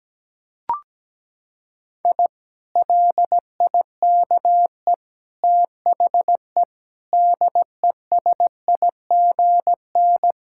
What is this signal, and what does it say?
Morse code, heard as a keyed tone, about 17 wpm, 700 Hz. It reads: I LIKE THE DESIGN
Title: Sentences from Top 1000 Words - Review Part 5 - 17wpm